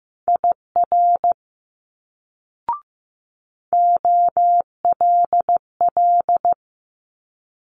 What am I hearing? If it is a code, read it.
Morse code, heard as a keyed tone, about 15 wpm, 700 Hz. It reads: IR OLL
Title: Multi-letter Phonemes Quick Fire 15wpm